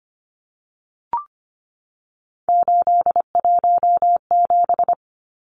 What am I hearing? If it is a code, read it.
Morse code, heard as a keyed tone, about 25 wpm, 700 Hz. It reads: 817
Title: Numbers - 3-Digit 25wpm